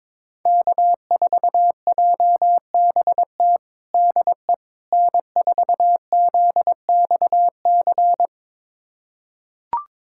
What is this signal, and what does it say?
Morse code, heard as a keyed tone, about 22 wpm, 700 Hz. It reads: K4JBT DE N4ZXC
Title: USA Callsign DE Callsign - 5 Characters - 22wpm